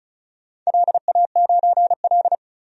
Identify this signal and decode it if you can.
Morse code, heard as a keyed tone, about 36 wpm, 690 Hz. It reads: LA9L
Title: DXCC-100 Call Signs - 4 Characters - 35wpm